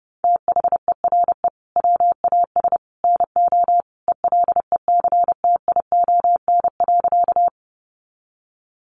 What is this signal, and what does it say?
Morse code, heard as a keyed tone, about 30 wpm, 700 Hz. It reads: THERE WAS NO ELECTION.
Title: Sentences from Top 1500 Words 30wpm